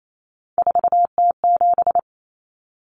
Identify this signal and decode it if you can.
Morse code, heard as a keyed tone, about 28 wpm, 700 Hz. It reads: 4T7
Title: Common Cut Numbers - 3-Digit 28wpm